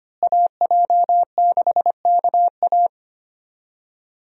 Morse code, heard as a keyed tone, about 25 wpm, 700 Hz. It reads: AJ6KA
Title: USA Call Signs - 5 Characters - Repeated 2x - 25wpm